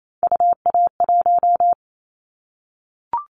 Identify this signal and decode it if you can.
Morse code, heard as a keyed tone, about 28 wpm, 700 Hz. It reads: UA1
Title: Callsign Prefixes for Top 100 DXCC - 28wpm